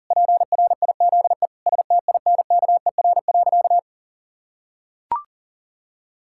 Morse code, heard as a keyed tone, about 40 wpm, 700 Hz. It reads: PRIZE STINKER.
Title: Pangram Challenge v2 - 40wpm